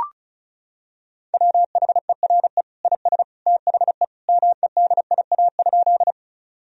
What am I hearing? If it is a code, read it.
Morse code, heard as a keyed tone, about 36 wpm, 700 Hz. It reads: WHERE IS THE MEDIA?
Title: Sentences from Top 1000 Words - Review Part 4 - 35wpm